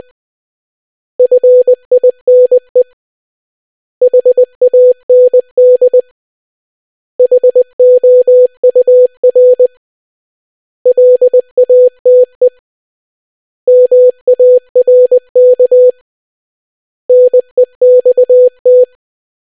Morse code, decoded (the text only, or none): FINE HAND HOUR LATE MARK NEXT